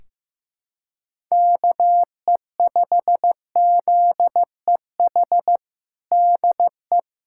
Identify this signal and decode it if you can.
Morse code, heard as a keyed tone, about 15 wpm, 700 Hz. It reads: KE5ZEH DE
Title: USA Callsign DE Callsign 15wpm